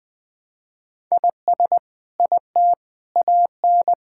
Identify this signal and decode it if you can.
Morse code, heard as a keyed tone, about 21 wpm, 700 Hz. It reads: IS IT AN
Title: Sentences from Top 1000 Words - Review Part 5 - 20wpm